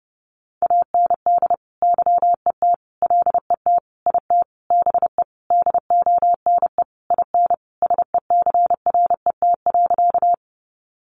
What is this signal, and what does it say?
Morse code, heard as a keyed tone, about 30 wpm, 700 Hz. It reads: AND YET LET IT BE DONE IN SECRET.
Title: Sentences from Top 1200 Words 30wpm